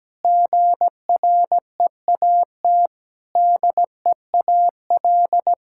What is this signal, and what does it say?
Morse code, heard as a keyed tone, about 17 wpm, 700 Hz. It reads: GREAT DEAL
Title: Sentences from Top 1000 Words - Review Part 6 - 17wpm